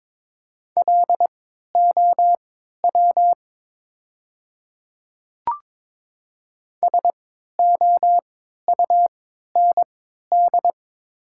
Morse code, heard as a keyed tone, about 22 wpm, spaced as for 13 wpm, 700 Hz. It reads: LOW SOUND
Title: 3x Character Spacing - Top 200 Words - 22wpm